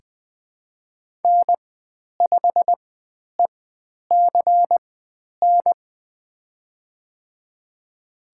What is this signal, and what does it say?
Morse code, heard as a keyed tone, about 20 wpm, spaced as for 10 wpm, 700 Hz. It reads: N5ECN
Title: 109 Lesson 18g - Letter m - Call Signs 20wpm